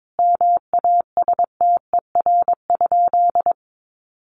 Morse code, heard as a keyed tone, about 22 wpm, 700 Hz. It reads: MASTER?
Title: Sentences from Top 1300 Words 22wpm